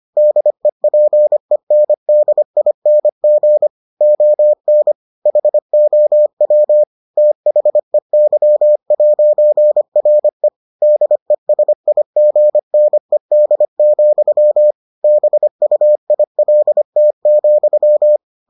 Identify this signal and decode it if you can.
Morse code, heard as a keyed tone, about 25 wpm, 600 Hz. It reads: DEPENDING ON HOW THEY'RE DESIGNED, BUILT,